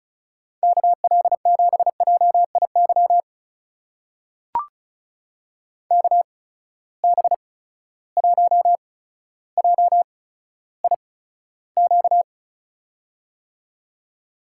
Morse code, heard as a keyed tone, about 35 wpm, spaced as for 15 wpm, 700 Hz. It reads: KL7JIY KB1JIQ
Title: USA Call Signs - 6 Characters - 8x Character Spacing - 35wpm